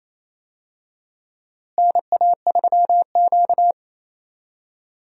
Morse code, heard as a keyed tone, about 28 wpm, 700 Hz. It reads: NA3Q
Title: USA Call Signs - 4 Characters - Repeated 3x - 28wpm